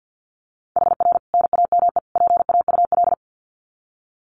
Morse code, heard as a keyed tone, about 48 wpm, 700 Hz. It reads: HR NAME PAUL
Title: Common QSO Phrases 50wpm